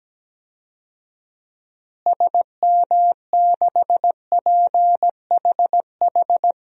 Morse code, heard as a keyed tone, about 17 wpm, 700 Hz. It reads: SM6PHH
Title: DXCC-100 Call Signs - 6 Characters - Repeated 2x - 17wpm